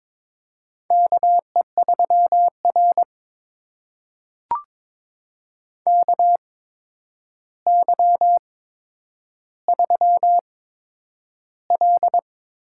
Morse code, heard as a keyed tone, about 22 wpm, spaced as for 9 wpm, 700 Hz. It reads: KE3R KY3L
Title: USA Call Signs - 4 Characters - 8x Character Spacing - 22wpm